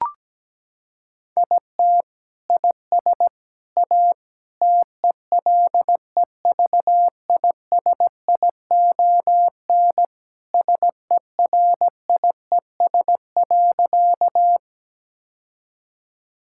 Morse code, heard as a keyed tone, about 17 wpm, 700 Hz. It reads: IT IS A TELEVISION SERIES.
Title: Sentences from Top 1000 Words - Review Part 5 - 17wpm